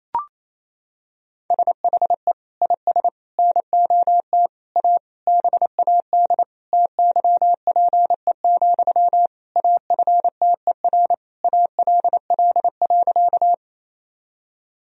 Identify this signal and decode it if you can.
Morse code, heard as a keyed tone, about 28 wpm, 700 Hz. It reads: SHE IS NOT A BAD TYPE, AFTER ALL.
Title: Sentences from Top 1000 Words - Review Part 3 - 28wpm